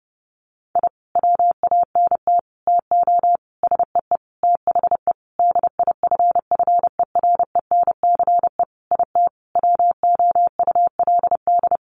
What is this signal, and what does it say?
Morse code, heard as a keyed tone, about 30 wpm, 700 Hz. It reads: I WANT TO SEE THE DIFFERENCE IT WOULD
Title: Sentences from Top 1300 Words 30wpm